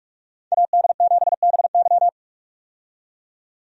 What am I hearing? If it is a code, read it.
Morse code, heard as a keyed tone, about 46 wpm, 700 Hz. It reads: AD7BY
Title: USA Call Signs - 5 Characters - 45wpm